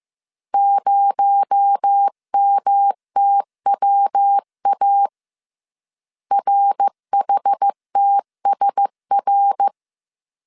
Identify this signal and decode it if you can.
Morse code, heard as a keyed tone, about 15 wpm, 800 Hz. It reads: 0MTWA RHTSR
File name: CW / DET / 1 6 1500C S C 14wpm 2951